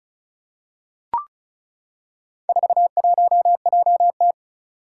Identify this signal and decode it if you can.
Morse code, heard as a keyed tone, about 36 wpm, 700 Hz. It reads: 41JT
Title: Four Challenging Letter-Number Combinations 35wpm